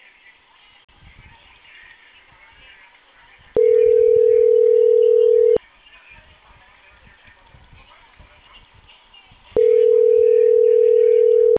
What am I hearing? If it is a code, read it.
Ringback tone (North America), 440 + 480 Hz.